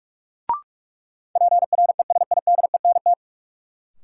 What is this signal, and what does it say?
Morse code, heard as a keyed tone, about 46 wpm, 700 Hz. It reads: PRESIDENT
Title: Sets of 2 Words - New Word Drill - 45wpm